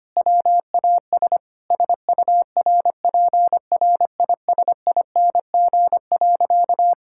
Morse code, heard as a keyed tone, about 25 wpm, 700 Hz. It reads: WAS SURPRISING.
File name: practice / 25wpm / Sentences with 4-Word Phrases - 25wpm